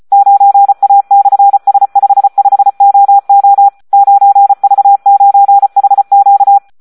Morse code, heard as a keyed tone, about 34 wpm, 800 Hz. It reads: 9A/S55OO 9V9HQ
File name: Audio practica CW / SD 1 34